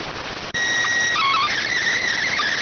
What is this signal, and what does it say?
SSTV (slow-scan television), PD 290.